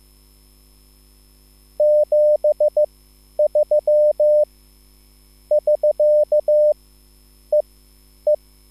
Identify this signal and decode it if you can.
Morse code, heard as a keyed tone, about 15 wpm, spaced as for 10 wpm, 600 Hz. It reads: 73 <SK> E E